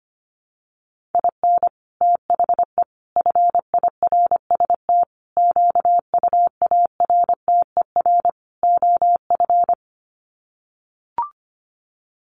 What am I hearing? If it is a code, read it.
Morse code, heard as a keyed tone, about 25 wpm, 700 Hz. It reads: IN THE FIRST QUARTER OF
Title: Sets of 5 Words Encore2 25wpm